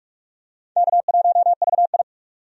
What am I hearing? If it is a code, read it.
Morse code, heard as a keyed tone, about 41 wpm, 700 Hz. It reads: K1VI
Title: USA Call Signs - 4 Characters - 45wpm